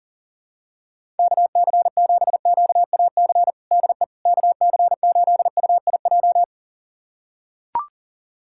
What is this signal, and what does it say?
Morse code, heard as a keyed tone, about 37 wpm, 700 Hz. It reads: KC7QAC DE KC8UIJ
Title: USA Callsign DE Callsign - 6 Characters - 40wpm